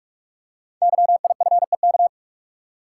Morse code, heard as a keyed tone, about 44 wpm, 690 Hz. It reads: YIFEK
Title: Code Groups - 5 Letters - 45wpm